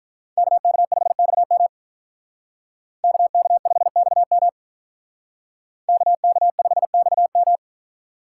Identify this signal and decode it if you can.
Morse code, heard as a keyed tone, about 45 wpm, 700 Hz. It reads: KK5XM KK5XM KK5XM